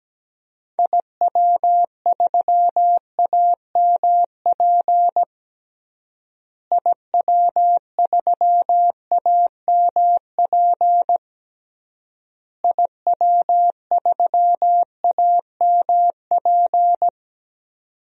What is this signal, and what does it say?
Morse code, heard as a keyed tone, about 17 wpm, 700 Hz. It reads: IW3AMP IW3AMP IW3AMP
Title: DXCC-100 Call Signs - 6 Characters - Repeated 3x - 17wpm